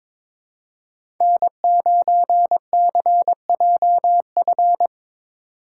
Morse code, heard as a keyed tone, about 22 wpm, 700 Hz. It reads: N9CJF
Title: USA Call Signs - 5 Characters - 3x Character Spacing - 22wpm